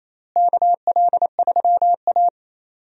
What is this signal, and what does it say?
Morse code, heard as a keyed tone, about 28 wpm, 700 Hz. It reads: KL3A
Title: USA Call Signs - 4 Characters - Repeated 2x - 28wpm